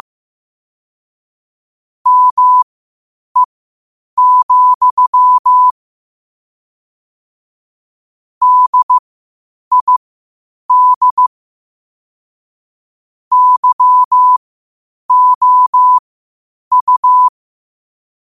Morse code, heard as a keyed tone, about 15 wpm, spaced as for 8 wpm, 1000 Hz. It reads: ME, DID YOU
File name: qotdfiles / 8wpm